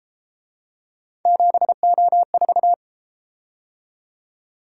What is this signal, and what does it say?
Morse code, heard as a keyed tone, about 33 wpm, 700 Hz. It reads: 7O4